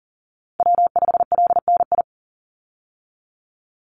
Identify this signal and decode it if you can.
Morse code, heard as a keyed tone, about 40 wpm, 700 Hz. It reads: W5LNI
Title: Call Signs - 40wpm